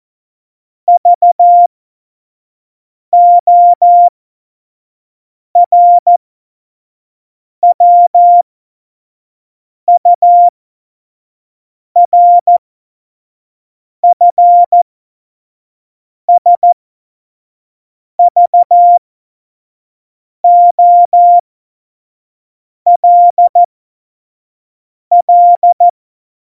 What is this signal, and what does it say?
Morse code, heard as a keyed tone, about 14 wpm, spaced as for 5 wpm, 700 Hz. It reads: VORWURFSVOLL
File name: Morsekurs / Witze / Witz 14. August 2018, 25 F70